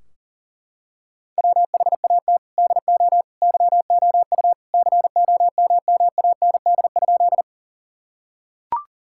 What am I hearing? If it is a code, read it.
Morse code, heard as a keyed tone, about 40 wpm, 700 Hz. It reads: WHAT DO YOU COMMAND?
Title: Sentences from Top 1600 Words 40wpm